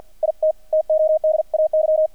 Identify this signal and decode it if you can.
Morse code, heard as a keyed tone, about 50 wpm, 630 Hz, with radio noise. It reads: IT TODAY
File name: archived / CRSnet